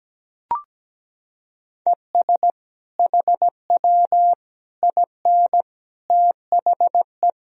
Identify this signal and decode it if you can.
Morse code, heard as a keyed tone, about 17 wpm, 700 Hz. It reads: ES HW IN THE
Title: Sentences from Top 200 Words and Abbreviations 17wpm